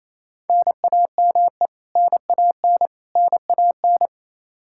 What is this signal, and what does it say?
Morse code, heard as a keyed tone, about 28 wpm, 700 Hz. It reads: NAME NAN NAN
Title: Practice QSO - Exchange 1 - Beginning - 28wpm